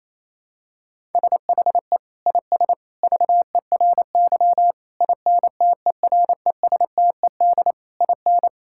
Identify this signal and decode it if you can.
Morse code, heard as a keyed tone, about 28 wpm, 700 Hz. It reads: SHE IS VERY INTERESTED IN
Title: Sentences from Top 1000 Words - Review Part 6 - 28wpm